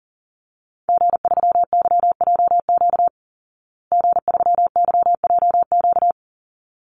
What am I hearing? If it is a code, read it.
Morse code, heard as a keyed tone, about 40 wpm, 700 Hz. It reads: G3YJQ G3YJQ
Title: Call Signs Repeated 3x - 40wpm